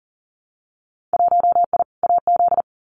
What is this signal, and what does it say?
Morse code, heard as a keyed tone, about 41 wpm, 700 Hz. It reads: 1I AZ
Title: Winter Field Day Exchanges 40wpm